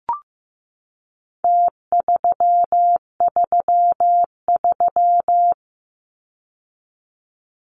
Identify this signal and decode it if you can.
Morse code, heard as a keyed tone, about 15 wpm, 700 Hz. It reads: T333
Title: Numbers - Contest Style - 4-Digit 15wpm